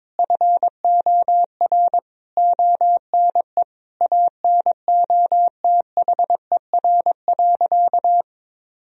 Morse code, heard as a keyed tone, about 22 wpm, 700 Hz. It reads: FOR ONE ANOTHER.
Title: Sentences from Top 1000 Words - Review Part 1 - 22wpm